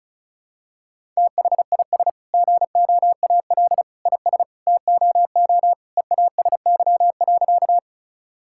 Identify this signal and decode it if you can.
Morse code, heard as a keyed tone, about 34 wpm, 700 Hz. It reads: THIS GOAL IS TOO EASY.
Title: Sentences from Top 1000 Words - Review Part 5 - 35wpm